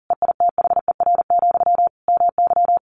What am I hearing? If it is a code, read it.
Morse code, heard as a keyed tone, about 40 wpm, 700 Hz. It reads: EITHER, MY
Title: Sentences from Top 1100 Words 40wpm